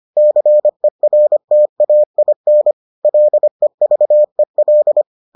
Morse code, heard as a keyed tone, about 25 wpm, 600 Hz. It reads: CERTAIN LEVEL